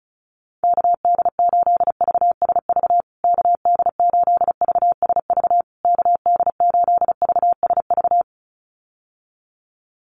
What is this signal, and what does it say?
Morse code, heard as a keyed tone, about 34 wpm, 700 Hz. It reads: KD8VSV KD8VSV KD8VSV
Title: USA Call Signs Repeated 3x 35wpm